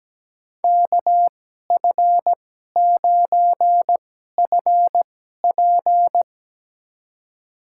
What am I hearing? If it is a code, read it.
Morse code, heard as a keyed tone, about 17 wpm, spaced as for 12 wpm, 700 Hz. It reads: KF9FP
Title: USA Call Signs - 5 Characters - 2x Character Spacing - 17wpm